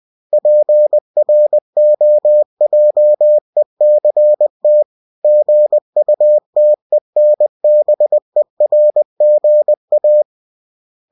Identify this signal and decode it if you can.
Morse code, heard as a keyed tone, about 20 wpm, 600 Hz. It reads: PROJECT GUTENBERG A
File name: winnie-20wpm